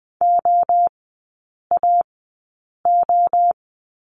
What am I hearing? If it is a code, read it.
Morse code, heard as a keyed tone, about 20 wpm, spaced as for 8 wpm, 700 Hz. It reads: OAO
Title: 041 Lesson 8e - Number 1 - Three Characters 20wpm